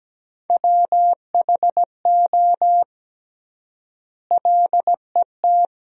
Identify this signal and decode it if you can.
Morse code, heard as a keyed tone, about 17 wpm, 700 Hz. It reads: WHO LET
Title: 3x Word-Spacing - Sentences from Top 300 Words - 17wpm